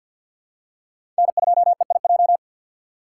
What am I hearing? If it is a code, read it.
Morse code, heard as a keyed tone, about 49 wpm, 700 Hz. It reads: N2EIJ
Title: USA Call Signs - 5 Characters - 2x Character Spacing - 50wpm